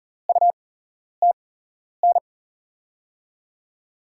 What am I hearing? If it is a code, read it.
Morse code, heard as a keyed tone, about 42 wpm, spaced as for 11 wpm, 700 Hz. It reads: UTN
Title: USA Call Signs - 6 Characters - 8x Character Spacing - 40wpm